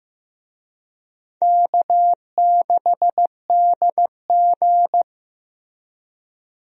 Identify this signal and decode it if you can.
Morse code, heard as a keyed tone, about 15 wpm, 700 Hz. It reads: K6DG
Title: USA Call Signs - 4 Characters - Repeated 3x - 15wpm